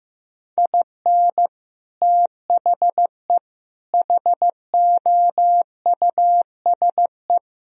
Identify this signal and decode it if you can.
Morse code, heard as a keyed tone, about 15 wpm, 700 Hz. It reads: IN THE HOUSE
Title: Sentences from Top 1000 Words - Review Part 2 - 15wpm